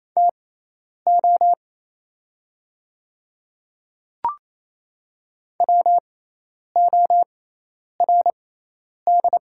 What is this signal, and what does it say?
Morse code, heard as a keyed tone, about 28 wpm, spaced as for 10 wpm, 700 Hz. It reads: TO WORD
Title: 6x Character Spacing - Top 200 Words - 28wpm